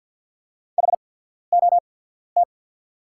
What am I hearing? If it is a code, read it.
Morse code, heard as a keyed tone, about 48 wpm, spaced as for 14 wpm, 700 Hz. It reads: HOT
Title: 8x Character Spacing - Top 200 Words - 50wpm